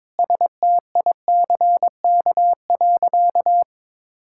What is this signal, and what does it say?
Morse code, heard as a keyed tone, about 22 wpm, 700 Hz. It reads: STICK.
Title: Sentences from Top 1600 Words 22wpm